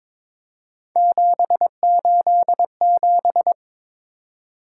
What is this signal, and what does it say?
Morse code, heard as a keyed tone, about 22 wpm, 700 Hz. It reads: 787
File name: practice / 22wpm / Three Letter-Number Combinations - 22wpm